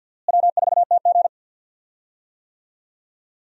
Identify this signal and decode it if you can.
Morse code, heard as a keyed tone, about 47 wpm, 700 Hz. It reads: W4TG
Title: USA Call Signs - 4 Characters - Repeated 3x - 50wpm